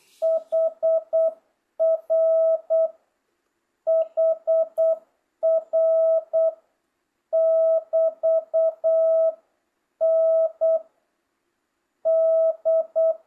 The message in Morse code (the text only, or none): HR HR <BT> N D